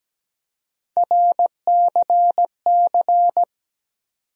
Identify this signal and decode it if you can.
Morse code, heard as a keyed tone, about 17 wpm, 700 Hz. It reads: RCC
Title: USA Call Sign Postfixes - 3 Letters 17wpm